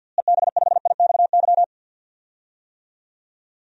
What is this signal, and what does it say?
Morse code, heard as a keyed tone, about 46 wpm, 700 Hz. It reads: EB5IXY
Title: DXCC-100 Call Signs - 6 Characters - Repeated 3x - 50wpm